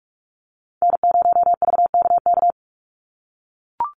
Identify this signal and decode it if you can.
Morse code, heard as a keyed tone, about 44 wpm, 700 Hz. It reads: N0VKK
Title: USA Call Signs 45wpm